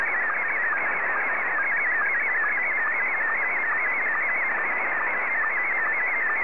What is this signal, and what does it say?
SSTV (slow-scan television), Scottie 1.
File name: sstv / S1-JA3OEN